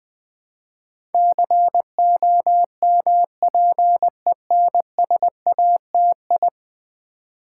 Morse code, heard as a keyed tone, about 20 wpm, 700 Hz. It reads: COMPENSATI
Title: Build A Word - 5 Letters At A Time - 12-Letter Words - 20wpm